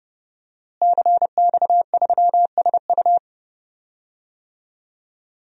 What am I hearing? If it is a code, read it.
Morse code, heard as a keyed tone, about 30 wpm, 700 Hz. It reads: CX3SU
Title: DXCC-100 Call Signs - 5 Characters - Repeated 3x - 30wpm